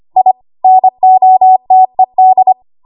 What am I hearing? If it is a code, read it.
Morse code, heard as a keyed tone, about 25 wpm, 750 Hz. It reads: I NOTED
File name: Morse / 25WPM